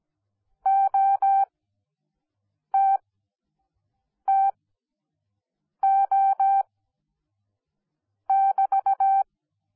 Morse code, heard as a keyed tone, about 17 wpm, spaced as for 5 wpm, 780 Hz. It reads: OTTO<BT>
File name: Onlinelehrgang / morsen / DJ4UF Morsen10